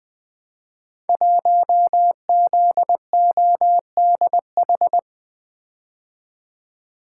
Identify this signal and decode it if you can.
Morse code, heard as a keyed tone, about 20 wpm, 700 Hz. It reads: 1ZODH
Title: Code Groups - 5 Letter:Numbers - 20wpm